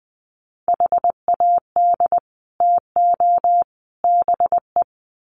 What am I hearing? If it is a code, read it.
Morse code, heard as a keyed tone, about 20 wpm, 700 Hz. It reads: HAD TO BE
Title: Sets of 3 Words 20wpm